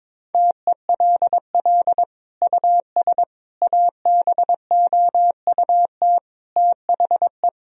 Morse code, heard as a keyed tone, about 21 wpm, 700 Hz. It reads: TELL US ABOUT THE
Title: Sentences from Top 1000 Words - Review Part 4 - 22wpm